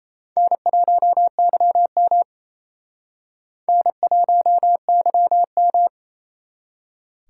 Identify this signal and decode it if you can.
Morse code, heard as a keyed tone, about 35 wpm, 700 Hz. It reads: N1YM N1YM
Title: Speed Racing - USA 1x2 Call Signs 28wpm